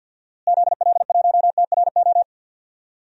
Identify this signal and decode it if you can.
Morse code, heard as a keyed tone, about 53 wpm, 700 Hz. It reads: ZL1TRO
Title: DXCC-100 Call Signs - 6 Characters - Repeated 2x - 50wpm